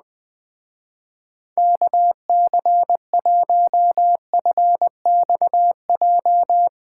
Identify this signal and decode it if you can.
Morse code, heard as a keyed tone, about 20 wpm, 700 Hz. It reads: KC1FXJ
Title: Speed Racing - USA 2x3 Call Signs 17wpm